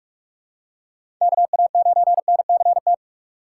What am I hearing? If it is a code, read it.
Morse code, heard as a keyed tone, about 46 wpm, 700 Hz. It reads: KA9NCT
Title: USA Call Signs - 6 Characters - 4x Character Spacing - 45wpm